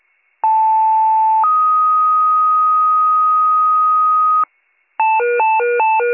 Two-tone page, heard as a one-second tone, then a three-second tone, about 880 then 1250 Hz. It is heard from 0.4 to 4.4 s.